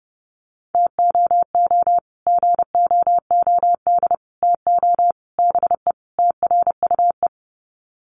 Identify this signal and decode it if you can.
Morse code, heard as a keyed tone, about 30 wpm, 700 Hz. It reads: TOO GOOD TO BE TRUE
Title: Sets of 5 Words Encore 30wpm